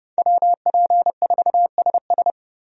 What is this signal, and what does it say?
Morse code, heard as a keyed tone, about 31 wpm, 700 Hz. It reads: WP4SS
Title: USA Call Signs - 5 Characters - 30wpm